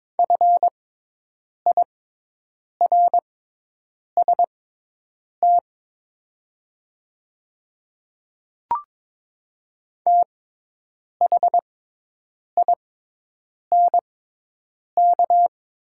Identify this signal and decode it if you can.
Morse code, heard as a keyed tone, about 22 wpm, spaced as for 8 wpm, 690 Hz. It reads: FIRST THINK
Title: 6x Character Spacing - Top 300 Words - 22wpm